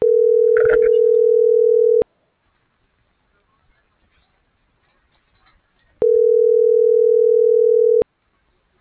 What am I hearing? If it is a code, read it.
Ringback tone (North America), 440 + 480 Hz.